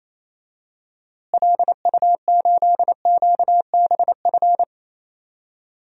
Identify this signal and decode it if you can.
Morse code, heard as a keyed tone, about 28 wpm, 700 Hz. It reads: LU8QBF
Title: DXCC-100 Call Signs - 6 Characters - Repeated 2x - 28wpm